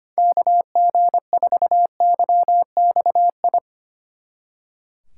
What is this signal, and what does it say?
Morse code, heard as a keyed tone, about 31 wpm, 700 Hz. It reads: KG4YXI
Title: Speed Racing - USA 2x3 Call Signs 25wpm